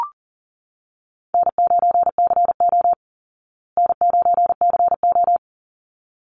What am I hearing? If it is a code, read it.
Morse code, heard as a keyed tone, about 40 wpm, 700 Hz. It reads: N9CO N9CO
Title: Easy Call Signs Repeated 2x - 40wpm